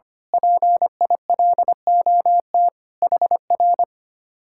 Morse code, heard as a keyed tone, about 24 wpm, 700 Hz. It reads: PILOT HR
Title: Speed-Racing Common QSO Phrases - 25wpm